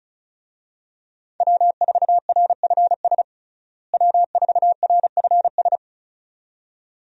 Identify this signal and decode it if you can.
Morse code, heard as a keyed tone, about 36 wpm, 700 Hz. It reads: W4RFS W4RFS
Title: USA Call Signs - 5 Characters - Repeated 2x - 35wpm